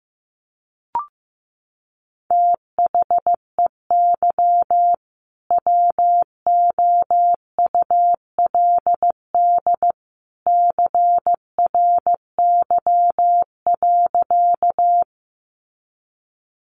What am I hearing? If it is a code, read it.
Morse code, heard as a keyed tone, about 15 wpm, 700 Hz. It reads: THEY WOULD CRY.
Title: Sentences from Top 500 Words 15wpm